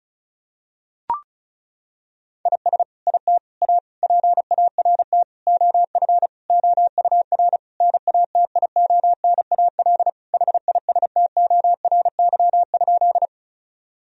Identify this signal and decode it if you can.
Morse code, heard as a keyed tone, about 34 wpm, 700 Hz. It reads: IS IT A PART OF OUR NATIONAL HISTORY?
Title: Sentences from Top 1000 Words - Review Part 4 - 35wpm